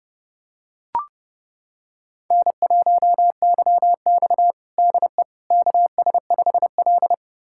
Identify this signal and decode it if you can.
Morse code, heard as a keyed tone, about 30 wpm, 700 Hz. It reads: N1YX DE KS5L
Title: USA Callsign DE Callsign - 4 Characters - 30wpm